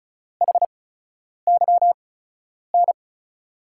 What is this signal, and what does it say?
Morse code, heard as a keyed tone, about 36 wpm, spaced as for 10 wpm, 700 Hz. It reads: HYN
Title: USA Call Signs - 6 Characters - 8x Character Spacing - 35wpm